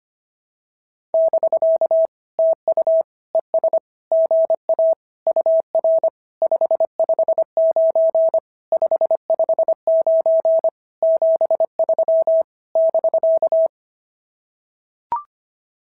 Morse code, heard as a keyed tone, about 24 wpm, 650 Hz. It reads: <BK> TU ES GA UR 559 559 73 <BK>
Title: SOTA QSO Practice 25wpm